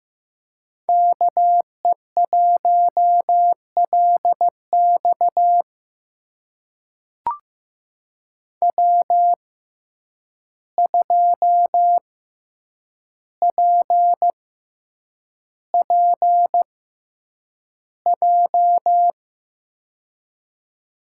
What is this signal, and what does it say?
Morse code, heard as a keyed tone, about 15 wpm, spaced as for 8 wpm, 700 Hz. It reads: KE1LX W2PPJ
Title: USA Call Signs - 5 Characters - 6x Character Spacing - 15wpm